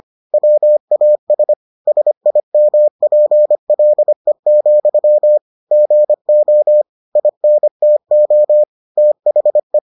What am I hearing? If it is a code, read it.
Morse code, heard as a keyed tone, about 25 wpm, 600 Hz. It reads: WAS SIMPLE, GO INTO THE